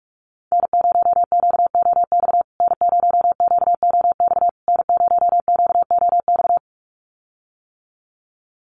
Morse code, heard as a keyed tone, about 44 wpm, 690 Hz. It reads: N0QOX N0QOX N0QOX
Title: USA Call Signs Repeated 3x 45wpm